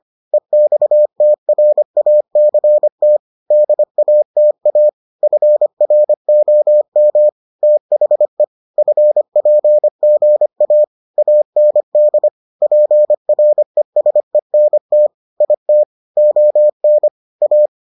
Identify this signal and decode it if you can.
Morse code, heard as a keyed tone, about 25 wpm, 600 Hz. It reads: EXTRACT DATA FROM THE FPGA AND PRESENT IT ON A